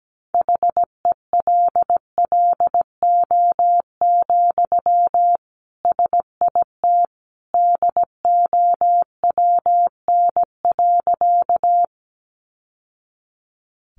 Morse code, heard as a keyed tone, about 17 wpm, 700 Hz. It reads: HELLO, SIT DOWN.
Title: Sentences from Top 1100 Words 17wpm